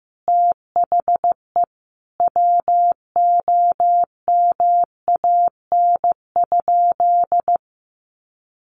Morse code, heard as a keyed tone, about 15 wpm, 700 Hz. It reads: THE WOMAN?
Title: Sentences from Top 1400 Words 15wpm